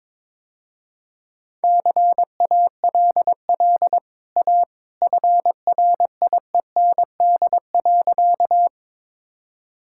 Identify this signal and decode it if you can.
Morse code, heard as a keyed tone, about 23 wpm, 700 Hz. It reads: CALL A FRIEND.
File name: practice / 22wpm / Easy Sentences with Hints - 22wpm